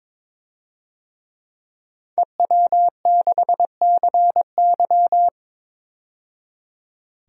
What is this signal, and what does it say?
Morse code, heard as a keyed tone, about 22 wpm, 700 Hz. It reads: EW6CY
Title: DXCC-100 Call Signs - 5 Characters - 22wpm